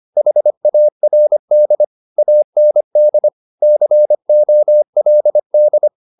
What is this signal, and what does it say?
Morse code, heard as a keyed tone, about 25 wpm, 600 Hz. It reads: HARD AND COLD
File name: the old man and the sea part 1 25wpm